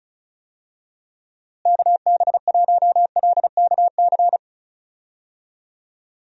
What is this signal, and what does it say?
Morse code, heard as a keyed tone, about 34 wpm, 700 Hz. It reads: KB1LKC
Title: USA Call Signs - 6 Characters - Repeated 2x - 35wpm